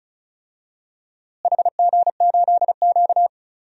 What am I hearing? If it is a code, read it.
Morse code, heard as a keyed tone, about 36 wpm, 700 Hz. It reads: HG8Q
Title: DXCC-100 Call Signs - 4 Characters - Repeated 2x - 35wpm